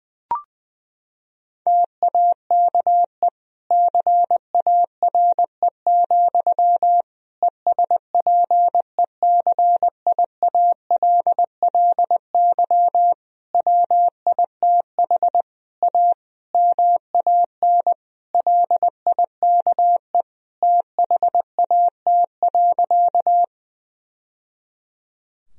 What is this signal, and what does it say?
Morse code, heard as a keyed tone, about 21 wpm, 700 Hz. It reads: TAKE CARE, ESPECIALLY WITH A MAN LIKE THAT.
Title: Sentences from Top 1000 Words - Review Part 4 - 20wpm